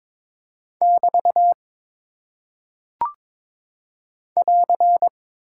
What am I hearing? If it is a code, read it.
Morse code, heard as a keyed tone, about 22 wpm, 700 Hz. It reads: <BT> <AR>
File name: practice / 22wpm / Single Punctuation-Prosign - 22wpm